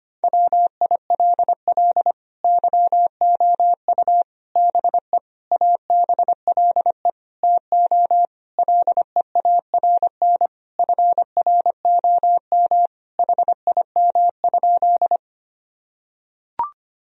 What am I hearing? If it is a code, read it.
Morse code, heard as a keyed tone, about 25 wpm, 700 Hz. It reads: WILL YOU BE ABLE TO LEARN FROM HIM?
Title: Sentences from Top 1000 Words - Review Part 2 - 25wpm